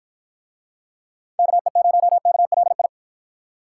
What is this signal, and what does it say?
Morse code, heard as a keyed tone, about 51 wpm, 700 Hz. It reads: KE0KLI